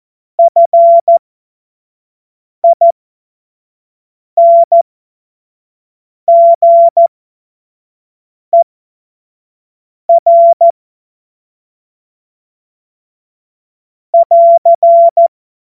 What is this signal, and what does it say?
Morse code, heard as a keyed tone, about 14 wpm, spaced as for 5 wpm, 680 Hz. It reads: FINGER <AR>
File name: Morsekurs / Witze / Witz 25. Mai 2021, 25 F70